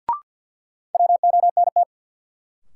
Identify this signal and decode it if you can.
Morse code, heard as a keyed tone, about 52 wpm, 700 Hz. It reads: WONT
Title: Sentences from Top 1000 Words - Review Part 6 - New Word Drill - 50wpm